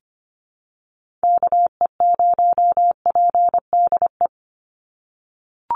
Morse code, heard as a keyed tone, about 25 wpm, 700 Hz. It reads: KE0PDE
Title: USA Call Signs 25wpm